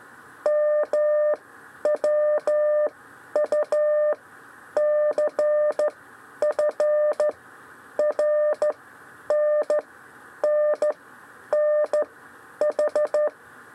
Morse code, heard as a keyed tone, about 14 wpm, spaced as for 9 wpm, 600 Hz. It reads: MWUCF RNNNH
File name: 6 lezione cw 1